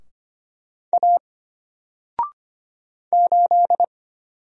Morse code, heard as a keyed tone, about 25 wpm, 700 Hz. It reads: A 8
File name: Single-Letter-Number-25wpm